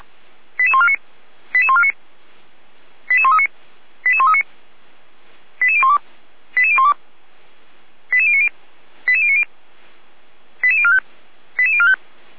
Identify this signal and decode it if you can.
Five-tone selective call (ZVEI), heard as quick runs of five tones, tones of about 70 ms. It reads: 88149 88149 88139 88139 88012 88012 88090 88090 88045 88045